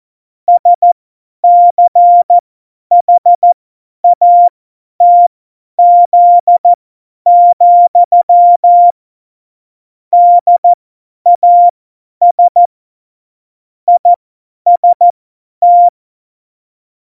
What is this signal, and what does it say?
Morse code, heard as a keyed tone, about 14 wpm, spaced as for 10 wpm, 700 Hz. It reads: SCHATZ, DAS IST
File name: Morsekurs / Witze / Witz 24.Dezember 2019, 50 F70